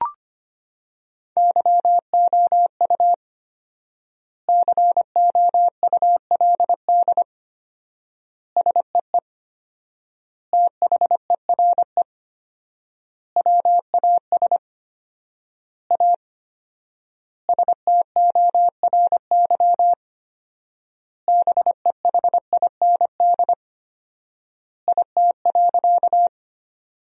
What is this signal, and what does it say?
Morse code, heard as a keyed tone, about 25 wpm, 700 Hz. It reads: YOU COULD SEE THERE WAS A STORY BEHIND IT.
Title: 4x Word-Spacing - Sentences from Top 500 Words - 25wpm